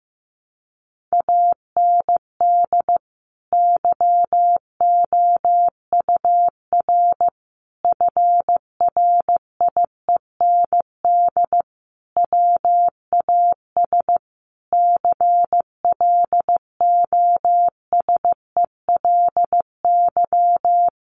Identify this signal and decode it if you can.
Morse code, heard as a keyed tone, about 15 wpm, 700 Hz. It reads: AND YOUR FRIEND WAS CLOSELY
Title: Sentences from Top 1400 Words 15wpm